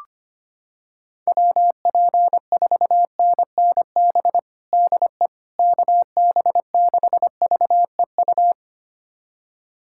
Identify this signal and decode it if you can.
Morse code, heard as a keyed tone, about 25 wpm, 700 Hz. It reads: WP4NNB DE KB6VEU
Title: USA Callsign DE Callsign - 6 Characters - 25wpm